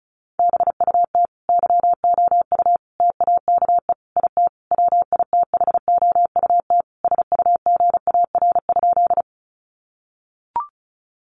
Morse code, heard as a keyed tone, about 34 wpm, 700 Hz. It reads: BUT YOU TAKE IT WITHOUT SUGAR?
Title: Sentences from Top 1200 Words 35wpm